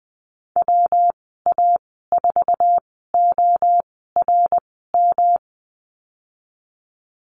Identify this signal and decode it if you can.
Morse code, heard as a keyed tone, about 19 wpm, spaced as for 14 wpm, 700 Hz. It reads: WA4ORM
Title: 161 Lesson 26g - Letter g - Call Signs 20wpm